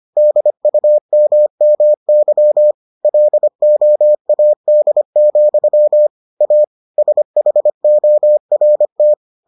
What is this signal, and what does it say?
Morse code, heard as a keyed tone, about 25 wpm, 600 Hz. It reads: DUMMY LOAD, A SHORT